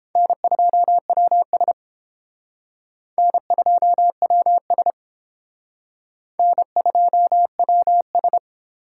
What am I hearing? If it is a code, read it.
Morse code, heard as a keyed tone, about 31 wpm, 700 Hz. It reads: N2WS N2WS N2WS